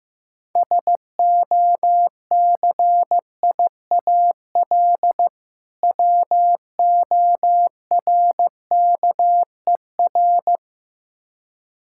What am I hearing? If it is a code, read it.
Morse code, heard as a keyed tone, about 15 wpm, 700 Hz. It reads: SOCIAL WORKER
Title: Practice QSO - Exchange 3 - More Info - 15wpm